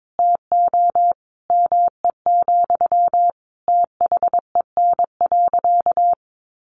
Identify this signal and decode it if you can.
Morse code, heard as a keyed tone, about 22 wpm, 700 Hz. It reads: TO ME, THEN.
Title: Sentences from Top 1100 Words 22wpm